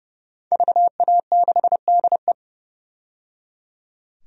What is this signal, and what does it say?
Morse code, heard as a keyed tone, about 30 wpm, 700 Hz. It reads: VA6DE
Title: DXCC-100 Call Signs - 5 Characters - 30wpm